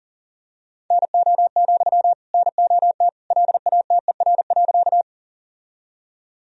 Morse code, heard as a keyed tone, about 37 wpm, 700 Hz. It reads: NO, NOT LATER.
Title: Sentences from Top 1000 Words - Review Part 4 - 40wpm